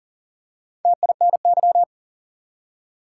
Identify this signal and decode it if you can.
Morse code, heard as a keyed tone, about 37 wpm, 700 Hz. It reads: TINY
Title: Four Letter Words - 40wpm